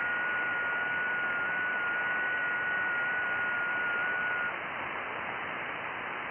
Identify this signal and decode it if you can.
FT8.
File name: IC-7100